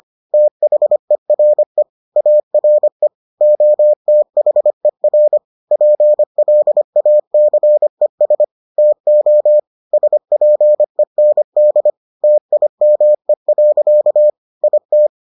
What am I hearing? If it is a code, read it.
Morse code, heard as a keyed tone, about 25 wpm, 600 Hz. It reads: THERE ARE OTHER PLACES TO SPEND TIME. IT